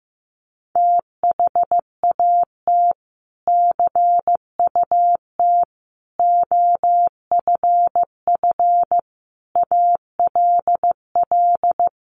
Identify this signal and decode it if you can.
Morse code, heard as a keyed tone, about 15 wpm, 700 Hz. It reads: THAT CUT OFF ALL
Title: Sentences from Top 1100 Words 15wpm